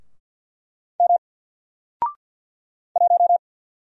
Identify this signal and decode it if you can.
Morse code, heard as a keyed tone, about 50 wpm, 700 Hz. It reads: M 1
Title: Single Letter-Number Alt 50wpm